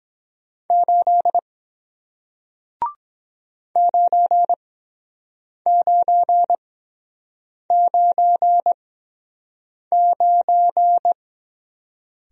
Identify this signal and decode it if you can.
Morse code, heard as a keyed tone, about 22 wpm, spaced as for 7 wpm, 700 Hz. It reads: 8 9999